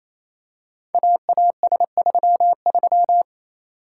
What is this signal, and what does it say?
Morse code, heard as a keyed tone, about 27 wpm, 700 Hz. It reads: AAS33
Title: Code Groups - 5 Letter:Numbers - 28wpm